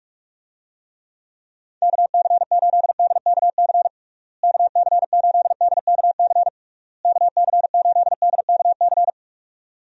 Morse code, heard as a keyed tone, about 46 wpm, 700 Hz. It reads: KC8DKC KC8DKC KC8DKC
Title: USA Call Signs - 6 Characters - Repeated 3x - 45wpm